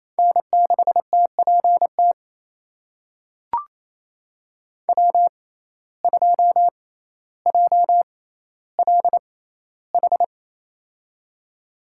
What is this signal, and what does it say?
Morse code, heard as a keyed tone, about 27 wpm, spaced as for 14 wpm, 700 Hz. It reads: N6TPT W2JLH
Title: USA Call Signs - 5 Characters - 6x Character Spacing - 28wpm